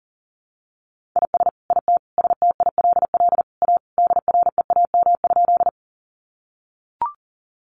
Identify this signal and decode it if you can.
Morse code, heard as a keyed tone, about 40 wpm, 700 Hz. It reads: IS IT STILL A DREAM?
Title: Sentences from Top 1500 Words 40wpm